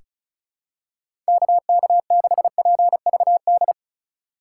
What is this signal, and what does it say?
Morse code, heard as a keyed tone, about 36 wpm, 700 Hz. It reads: KK6PVD
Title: USA Call Signs - 6 Characters - 4x Character Spacing - 35wpm